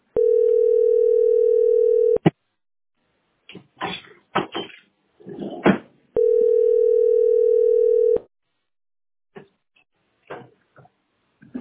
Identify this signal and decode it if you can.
Ringback tone (North America), 440 + 480 Hz.